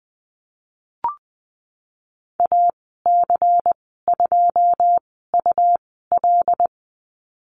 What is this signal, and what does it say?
Morse code, heard as a keyed tone, about 20 wpm, spaced as for 15 wpm, 700 Hz. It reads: AC2UL
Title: 128 Lesson 21g - Number 6 - Call Signs 20wpm